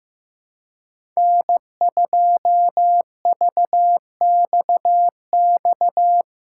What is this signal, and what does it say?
Morse code, heard as a keyed tone, about 15 wpm, 700 Hz. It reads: N2VXX
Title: USA Callsign DE Callsign - 5 Characters - 15wpm